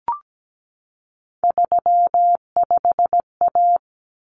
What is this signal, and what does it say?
Morse code, heard as a keyed tone, about 17 wpm, 690 Hz. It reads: 35A
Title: Common Cut Numbers - 3-Digit 17wpm